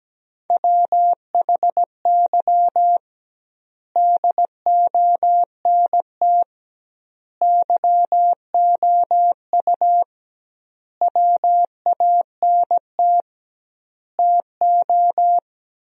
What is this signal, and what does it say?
Morse code, heard as a keyed tone, about 17 wpm, 700 Hz. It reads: WHY DONT YOU WANT TO
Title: 2x Word-Spacing - Sentences from Top 300 Words - 17wpm